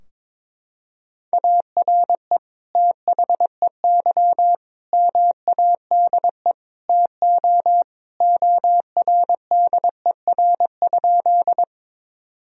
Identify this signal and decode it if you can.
Morse code, heard as a keyed tone, about 22 wpm, 700 Hz. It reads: ARE THEY MADE TO ORDER?
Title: Sentences from Top 1000 Words - Review Part 2 - 22wpm